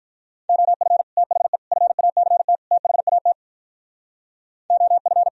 Morse code, heard as a keyed tone, about 45 wpm, 700 Hz. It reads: OF THE FACT THAT OF